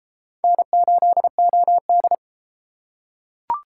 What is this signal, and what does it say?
Morse code, heard as a keyed tone, about 30 wpm, 700 Hz. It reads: N8OD